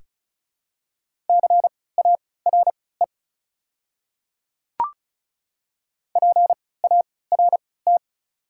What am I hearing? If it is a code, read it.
Morse code, heard as a keyed tone, about 34 wpm, spaced as for 20 wpm, 700 Hz. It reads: CARE PART
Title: 3x Character Spacing - Top 300 Words - 35wpm